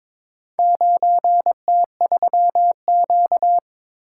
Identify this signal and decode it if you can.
Morse code, heard as a keyed tone, about 23 wpm, 700 Hz. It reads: 9T3Q
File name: practice / 22wpm / Four Letter-Number Combinations - 22wpm